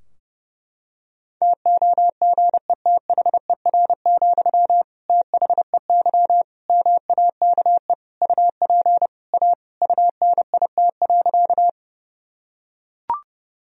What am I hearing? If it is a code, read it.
Morse code, heard as a keyed tone, about 30 wpm, 700 Hz. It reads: TOGETHER, THEY MAKE UP A UNIT.
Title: Sentences from Top 1000 Words - Review Part 2 - 30wpm